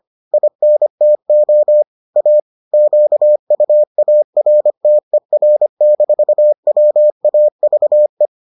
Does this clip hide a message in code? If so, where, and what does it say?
Yes: Morse code from 0.3 to 8.3 s, heard as a keyed tone, about 25 wpm, 600 Hz. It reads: INTO A QUARTER-WAVE